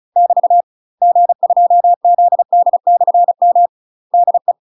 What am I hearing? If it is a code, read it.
Morse code, heard as a keyed tone, about 36 wpm, 700 Hz. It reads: <BT> G2ZD/M DE
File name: UK MASTER QSOs TRAINING / 35WPM / UK MASTER QSOs 0020